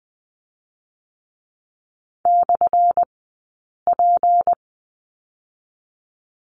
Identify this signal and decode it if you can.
Morse code, heard as a keyed tone, about 20 wpm, spaced as for 8 wpm, 700 Hz. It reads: /P
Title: 211 Lesson 34d - Number 8 - Two Characters 20wpm